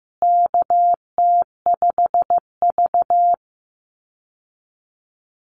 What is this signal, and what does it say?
Morse code, heard as a keyed tone, about 15 wpm, 700 Hz. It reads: KT5V
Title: Call Signs Repeated 2x - 15wpm